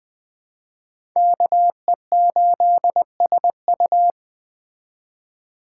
Morse code, heard as a keyed tone, about 21 wpm, 700 Hz. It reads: KE8SU
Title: Speed Racing - USA Call Signs 17wpm